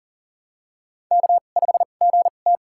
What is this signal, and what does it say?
Morse code, heard as a keyed tone, about 40 wpm, spaced as for 29 wpm, 700 Hz. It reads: K5GT
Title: USA Call Signs - 4 Characters - 2x Character Spacing - 40wpm